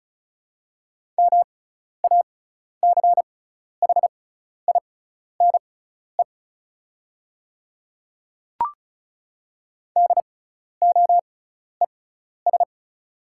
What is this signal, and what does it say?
Morse code, heard as a keyed tone, about 34 wpm, spaced as for 12 wpm, 690 Hz. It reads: MACHINE DOES
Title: 6x Character Spacing - Top 500 Words - 35wpm